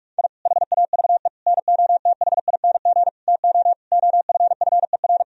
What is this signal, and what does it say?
Morse code, heard as a keyed tone, about 46 wpm, 700 Hz. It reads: I HAVE NOTHING TO OFFER
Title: Sentences from Top 1000 Words - Review Part 4 - 45wpm